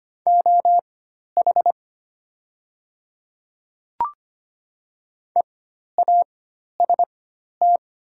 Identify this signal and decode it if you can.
Morse code, heard as a keyed tone, about 25 wpm, spaced as for 12 wpm, 700 Hz. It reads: OH EAST
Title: 4x Character Spacing - Top 500 Words - 25wpm